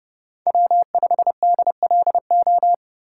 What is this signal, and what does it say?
Morse code, heard as a keyed tone, about 30 wpm, 700 Hz. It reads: W5DLO
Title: USA Call Signs - 5 Characters - 2x Character Spacing - 30wpm